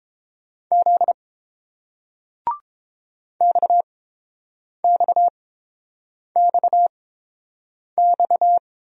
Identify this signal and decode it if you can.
Morse code, heard as a keyed tone, about 30 wpm, spaced as for 7 wpm, 700 Hz. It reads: Z XXXX